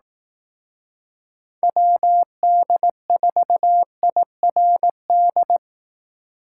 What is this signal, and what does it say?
Morse code, heard as a keyed tone, about 18 wpm, 700 Hz. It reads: WD4IRD
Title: Speed Racing - USA 2x3 Call Signs 15wpm